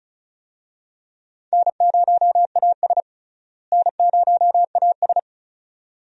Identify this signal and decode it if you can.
Morse code, heard as a keyed tone, about 35 wpm, 700 Hz. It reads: N0AS N0AS
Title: USA Call Signs - 4 Characters - Repeated 2x - 35wpm